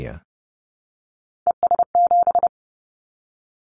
Morse code, heard as a keyed tone, about 30 wpm, 700 Hz. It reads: ES7
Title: Callsign Prefixes for Top 100 DXCC - 30wpm